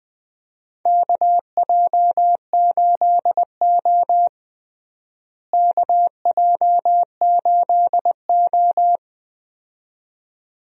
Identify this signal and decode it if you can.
Morse code, heard as a keyed tone, about 20 wpm, 700 Hz. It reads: KJ8O KJ8O
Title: USA Call Signs - 4 Characters - Repeated 2x - 20wpm